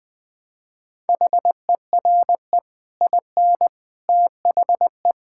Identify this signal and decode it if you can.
Morse code, heard as a keyed tone, about 20 wpm, 700 Hz. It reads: HERE IN THE
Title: Build A Phrase - 5-Word Phrases - 20wpm